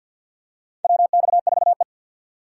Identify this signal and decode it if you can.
Morse code, heard as a keyed tone, about 52 wpm, 700 Hz. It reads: WX4E
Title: USA Call Signs - 4 Characters - 3x Character Spacing - 50wpm